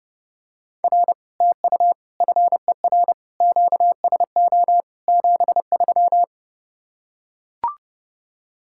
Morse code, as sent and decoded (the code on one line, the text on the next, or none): .-. / - ..- / ..-. .-. / --.- ... --- / --... ...--
R TU FER QSO 73